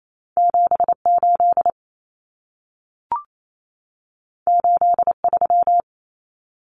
Morse code, heard as a keyed tone, about 28 wpm, 700 Hz. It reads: 78 83
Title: Numbers - 2-Digit 28wpm